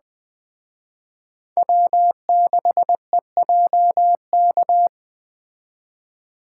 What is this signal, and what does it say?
Morse code, heard as a keyed tone, about 20 wpm, 700 Hz. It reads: W6EJK